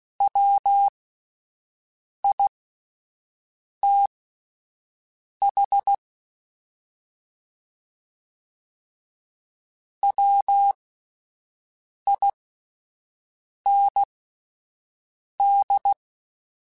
Morse code, heard as a keyed tone, about 16 wpm, spaced as for 5 wpm, 800 Hz. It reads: WITH WIND
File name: Spaced Characters / Spaced Characters-mp3s / Weather 5 1 sp @10